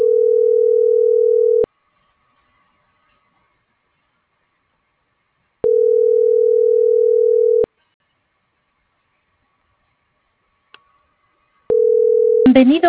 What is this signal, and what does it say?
Ringback tone (North America), 440 + 480 Hz.